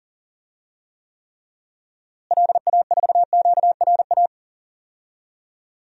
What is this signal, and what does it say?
Morse code, heard as a keyed tone, about 40 wpm, 700 Hz. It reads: LA4QRA
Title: DXCC-100 Call Signs - 6 Characters - Repeated 3x - 40wpm